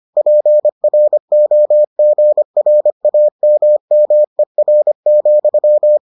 Morse code, heard as a keyed tone, about 25 wpm, 600 Hz. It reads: PROGRAMMER,